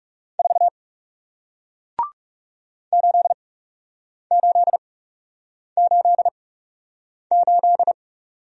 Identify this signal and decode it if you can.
Morse code, heard as a keyed tone, about 40 wpm, spaced as for 8 wpm, 700 Hz. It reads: 4 8888